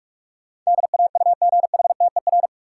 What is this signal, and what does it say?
Morse code, heard as a keyed tone, about 46 wpm, 700 Hz. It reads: DAUGHTER
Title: Sentences from Top 1500 Words - New Word Drill - 45wpm